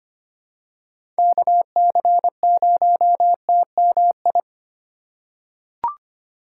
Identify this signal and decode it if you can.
Morse code, heard as a keyed tone, about 25 wpm, 700 Hz. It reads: KC0TMI
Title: USA Call Signs - 6 Characters - 2x Character Spacing - 25wpm